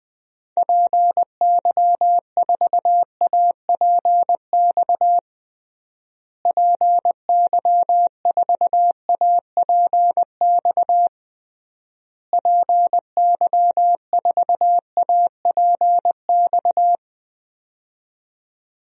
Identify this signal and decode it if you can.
Morse code, heard as a keyed tone, about 20 wpm, 700 Hz. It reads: PY4APX PY4APX PY4APX
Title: DXCC-100 Call Signs - 6 Characters - Repeated 3x - 20wpm